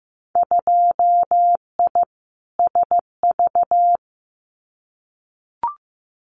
Morse code, heard as a keyed tone, about 15 wpm, 700 Hz. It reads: 2I SV
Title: Winter Field Day Exchanges Encore 15wpm